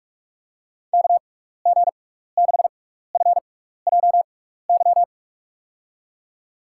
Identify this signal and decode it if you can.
Morse code, heard as a keyed tone, about 46 wpm, spaced as for 16 wpm, 700 Hz. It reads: KG6FJY
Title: USA Call Signs - 6x Character Spacing - 45wpm